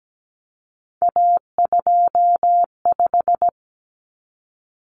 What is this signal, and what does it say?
Morse code, heard as a keyed tone, about 17 wpm, 690 Hz. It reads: A25
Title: Common Cut Numbers - 3-Digit 17wpm